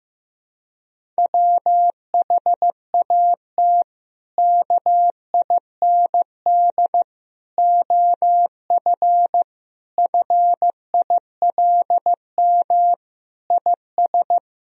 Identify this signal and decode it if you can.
Morse code, heard as a keyed tone, about 15 wpm, 700 Hz. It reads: WHAT KIND OF FILM IS
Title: Sentences from Top 1000 Words - Review Part 3 - 15wpm